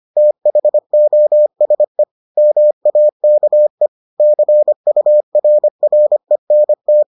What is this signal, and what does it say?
Morse code, heard as a keyed tone, about 25 wpm, 600 Hz. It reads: THOSE MAKE CURRENT